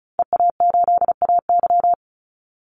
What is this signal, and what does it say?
Morse code, heard as a keyed tone, about 34 wpm, 700 Hz. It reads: EA8AY
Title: Call Signs - 35wpm